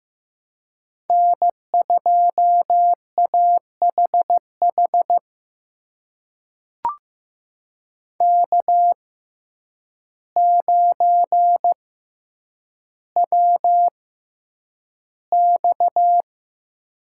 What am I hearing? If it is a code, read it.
Morse code, heard as a keyed tone, about 15 wpm, spaced as for 8 wpm, 700 Hz. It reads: N2AHH K9WX
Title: USA Call Signs - 6x Character Spacing - 15wpm